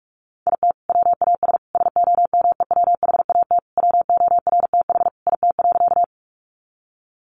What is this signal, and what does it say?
Morse code, heard as a keyed tone, about 44 wpm, 700 Hz. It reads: IT WAS SOMEWHAT WORTH IT.
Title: Sentences from Top 1300 Words 45wpm